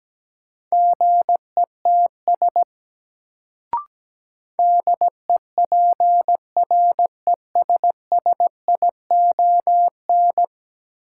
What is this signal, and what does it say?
Morse code, heard as a keyed tone, about 17 wpm, 700 Hz. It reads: GETS DEPRESSION
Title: Sentences from Top 1300 Words - New Word Drill - 17wpm